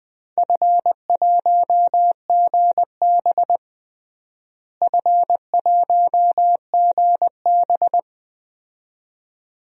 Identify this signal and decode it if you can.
Morse code, heard as a keyed tone, about 20 wpm, 700 Hz. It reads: F1GB F1GB
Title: DXCC-100 Call Signs - 4 Characters - Repeated 2x - 20wpm